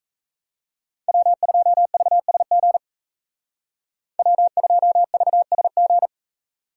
Morse code, heard as a keyed tone, about 36 wpm, 700 Hz. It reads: W2VSG W2VSG